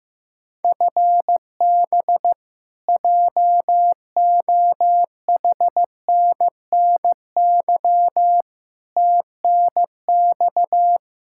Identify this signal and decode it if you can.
Morse code, heard as a keyed tone, about 15 wpm, 700 Hz. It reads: FB JOHNNY TNX
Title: Practice QSO - Exchange 3 - More Info - 15wpm